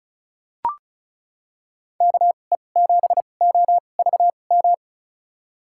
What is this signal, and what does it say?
Morse code, heard as a keyed tone, about 36 wpm, spaced as for 26 wpm, 690 Hz. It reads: KE7OVM
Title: USA Call Signs - 6 Characters - 2x Character Spacing - 35wpm